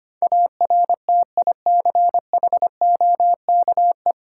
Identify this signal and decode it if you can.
Morse code, heard as a keyed tone, about 25 wpm, 700 Hz. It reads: ARTICHOKE
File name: practice / 25wpm / Difficult Words with Hints - 25wpm